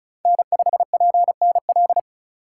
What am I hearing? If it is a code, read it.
Morse code, heard as a keyed tone, about 34 wpm, 700 Hz. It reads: N5PNL
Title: USA Call Signs - 5 Characters - 3x Character Spacing - 35wpm